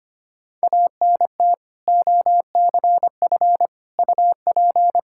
Morse code, heard as a keyed tone, about 25 wpm, 700 Hz. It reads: ANT OCF UP
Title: K1USN SST QSO Practice 25wpm